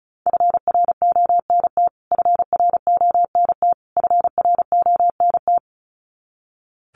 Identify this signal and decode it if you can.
Morse code, heard as a keyed tone, about 34 wpm, 700 Hz. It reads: FRONT FRONT FRONT
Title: Top 500 Words Repeated 35wpm